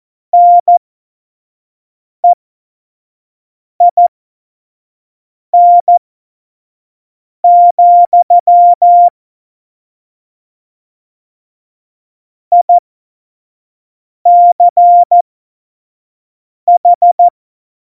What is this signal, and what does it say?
Morse code, heard as a keyed tone, about 14 wpm, spaced as for 5 wpm, 700 Hz. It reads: NEIN, ICH
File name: Morsekurs / Witze / Witz 10. Oktober 17, Tempo 25